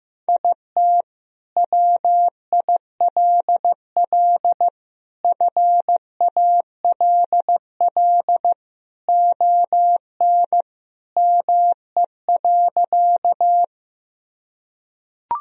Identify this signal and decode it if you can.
Morse code, heard as a keyed tone, about 15 wpm, 700 Hz. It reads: IT WILL FALL ON ME.
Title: Sentences from Top 1000 Words - Review Part 2 - 15wpm